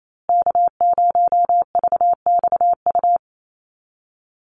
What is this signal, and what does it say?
Morse code, heard as a keyed tone, about 28 wpm, 700 Hz. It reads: K0VXU
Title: Call Signs Repeated 3x - 28wpm